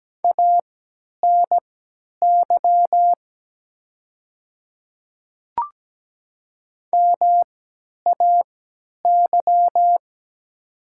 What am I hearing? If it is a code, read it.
Morse code, heard as a keyed tone, about 17 wpm, spaced as for 10 wpm, 700 Hz. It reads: ANY MAY
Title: 3x Character Spacing - Top 500 Words - 17wpm